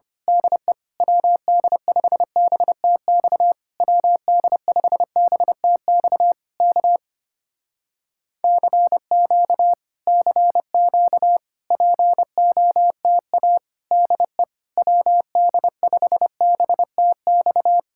Morse code, heard as a keyed tone, about 24 wpm, 700 Hz. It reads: DE WD5BTX WD5BTX K CQ CQ POTA DE WD5BTX
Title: Speed-Racing Common QSO Phrases Encore - 25wpm